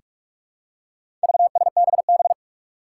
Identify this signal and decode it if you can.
Morse code, heard as a keyed tone, about 46 wpm, 700 Hz. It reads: VSBB
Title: Four Challenging Letter-Number Combinations 45wpm